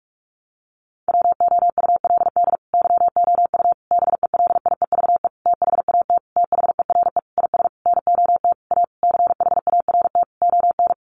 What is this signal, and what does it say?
Morse code, heard as a keyed tone, about 44 wpm, 700 Hz. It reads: WOULD YOU BELIEVE THAT THERE IS NOT A CHART ON
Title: Sentences from Top 1300 Words 45wpm